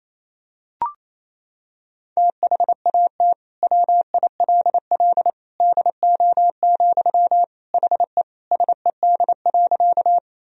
Morse code, heard as a keyed tone, about 28 wpm, 700 Hz. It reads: THAT WILL DO, HE SED.
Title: Sentences from Top 100 Words and Abbreviations 28wpm